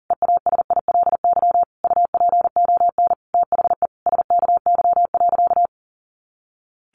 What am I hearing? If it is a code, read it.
Morse code, heard as a keyed tone, about 40 wpm, 700 Hz. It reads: EASILY UPON THE SKY.
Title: A Sound Of Thunder 40wpm